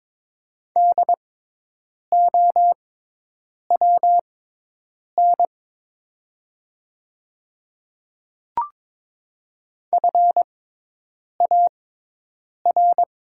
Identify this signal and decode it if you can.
Morse code, heard as a keyed tone, about 22 wpm, spaced as for 8 wpm, 700 Hz. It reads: DOWN FAR
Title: 6x Character Spacing - Top 500 Words - 22wpm